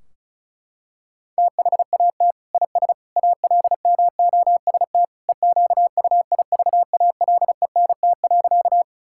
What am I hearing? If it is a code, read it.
Morse code, heard as a keyed tone, about 36 wpm, 700 Hz. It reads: THAT IS ALMOST EQUIVALENT.
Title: Sentences from Top 1600 Words 35wpm